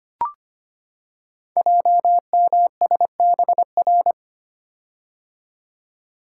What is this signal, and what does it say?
Morse code, heard as a keyed tone, about 25 wpm, 700 Hz. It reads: JMSBR
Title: Code Groups - 5 Letters - 25wpm